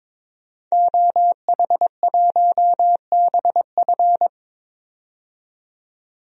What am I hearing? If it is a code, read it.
Morse code, heard as a keyed tone, about 21 wpm, 700 Hz. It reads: OH1BF
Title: DXCC-100 Call Signs - 5 Characters - 22wpm